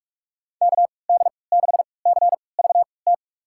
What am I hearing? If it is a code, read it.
Morse code, heard as a keyed tone, about 41 wpm, spaced as for 25 wpm, 700 Hz. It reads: KD6CVT
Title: USA Call Signs - 6 Characters - 3x Character Spacing - 45wpm